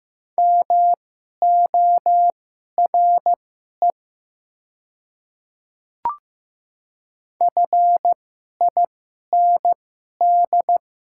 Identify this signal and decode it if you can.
Morse code, heard as a keyed tone, about 15 wpm, spaced as for 11 wpm, 700 Hz. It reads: MORE FIND
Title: 2x Character Spacing - Top 300 Words - 15wpm